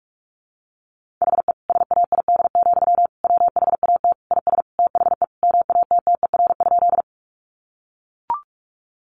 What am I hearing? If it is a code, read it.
Morse code, heard as a keyed tone, about 44 wpm, 700 Hz. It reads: HE SAID, WHAT IS THE MATTER?
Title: Sentences from Top 700 Words 45wpm